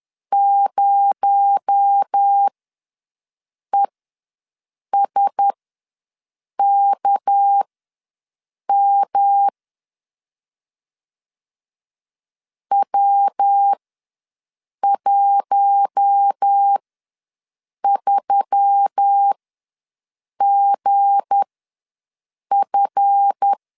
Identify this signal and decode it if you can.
Morse code, heard as a keyed tone, about 10 wpm, spaced as for 6 wpm, 800 Hz. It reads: 0ESKM W13GF
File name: LEZIONI / 7 GF9Z / 7 GF ws 10wpm 3918